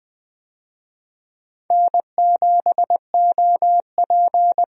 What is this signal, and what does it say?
Morse code, heard as a keyed tone, about 20 wpm, 700 Hz. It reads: N7OP
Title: USA Call Signs - 4 Characters - 20wpm